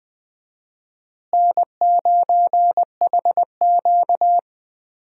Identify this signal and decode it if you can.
Morse code, heard as a keyed tone, about 20 wpm, 700 Hz. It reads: N9HQ
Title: USA Call Signs - 4 Characters - Repeated 3x - 20wpm